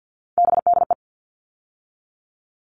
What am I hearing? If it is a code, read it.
Morse code, heard as a keyed tone, about 50 wpm, 700 Hz. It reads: BDE
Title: All Cut Numbers - 3-Digit 50wpm